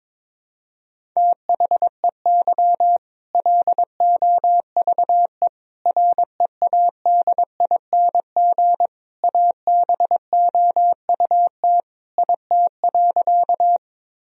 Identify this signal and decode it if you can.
Morse code, heard as a keyed tone, about 21 wpm, 700 Hz. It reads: THEY LOVE READING ABOUT IT.
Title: Sentences from Top 1000 Words - Review Part 3 - 22wpm